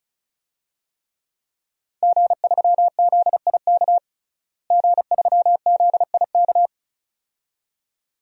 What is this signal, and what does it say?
Morse code, heard as a keyed tone, about 35 wpm, 700 Hz. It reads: G3ZIK G3ZIK
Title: DXCC-100 Call Signs - 5 Characters - Repeated 2x - 35wpm